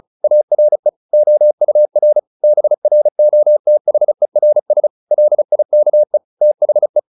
Morse code, heard as a keyed tone, about 34 wpm, 600 Hz. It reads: ARE OUR BROTHERS LIKE THE
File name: the old man and the sea / the old man and the sea 1 35wpm